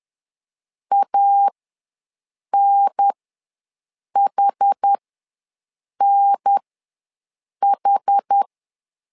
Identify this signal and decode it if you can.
Morse code, heard as a keyed tone, about 10 wpm, spaced as for 6 wpm, 800 Hz. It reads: ANHNH
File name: 2 UHDN ws 10wpm